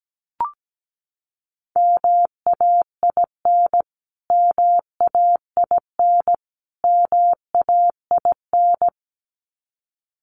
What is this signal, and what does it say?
Morse code, heard as a keyed tone, about 17 wpm, 700 Hz. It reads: MAIN MAIN MAIN
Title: Top 500 Words Repeated 17wpm